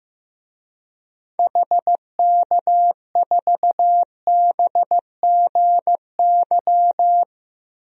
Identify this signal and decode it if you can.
Morse code, heard as a keyed tone, about 15 wpm, 700 Hz. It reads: HK4BGY
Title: DXCC-100 Call Signs - 6 Characters - Repeated 3x - 15wpm